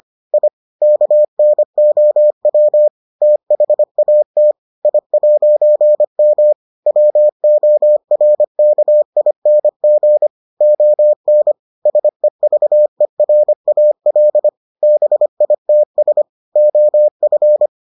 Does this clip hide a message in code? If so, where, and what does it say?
Yes: Morse code from 0.3 to 17.7 s, heard as a keyed tone, about 25 wpm, 600 Hz. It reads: I KNOW THAT I'M WORKING ON SEVERAL BITS OF